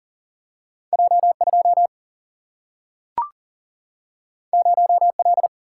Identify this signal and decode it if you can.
Morse code, heard as a keyed tone, about 40 wpm, 700 Hz. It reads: J 2 0 L
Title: Single-Letter Single-Number 40wpm